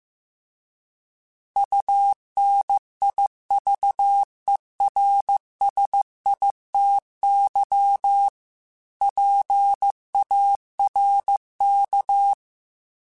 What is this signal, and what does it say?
Morse code, heard as a keyed tone, about 15 wpm, 800 Hz. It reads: UNIVERSITY PARK